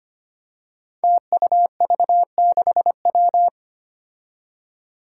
Morse code, heard as a keyed tone, about 25 wpm, 700 Hz. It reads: TUV6W
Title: Five Challenging Letter-Number Combinations 25wpm